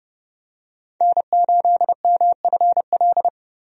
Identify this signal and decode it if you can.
Morse code, heard as a keyed tone, about 30 wpm, 700 Hz. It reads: N8MFL
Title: USA Call Signs - 5 Characters - 30wpm